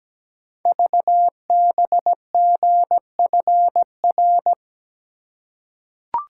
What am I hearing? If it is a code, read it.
Morse code, heard as a keyed tone, about 17 wpm, 700 Hz. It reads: VBGFR
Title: Code Groups - 5 Letters - 17wpm